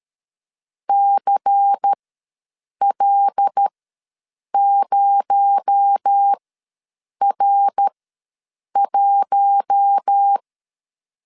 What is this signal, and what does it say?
Morse code, heard as a keyed tone, about 13 wpm, spaced as for 7 wpm, 800 Hz. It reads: CL0R1
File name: LEZIONI / 7 GF9Z / 7 G ws 12wpm 3140